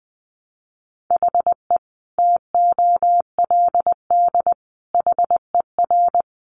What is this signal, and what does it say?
Morse code, heard as a keyed tone, about 20 wpm, 700 Hz. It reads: HE TOLD HER
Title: Sentences from Top 700 Words 20wpm